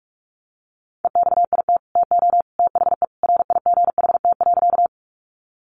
Morse code, heard as a keyed tone, about 44 wpm, 700 Hz. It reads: EXIT TO THE RIGHT.
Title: Sentences from Top 1100 Words 45wpm